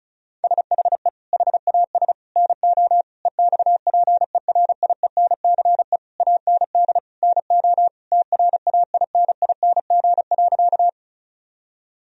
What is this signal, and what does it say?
Morse code, heard as a keyed tone, about 34 wpm, 700 Hz. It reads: SHE HAS NO EXPERIENCE AND NO TRAINING.
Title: Sentences from Top 1000 Words - Review Part 4 - 35wpm